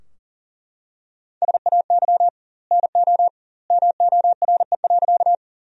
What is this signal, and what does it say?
Morse code, heard as a keyed tone, about 40 wpm, 700 Hz. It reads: SAY NO MORE.
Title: 2x Word-Spacing - Sentences from Top 200 Words - 40wpm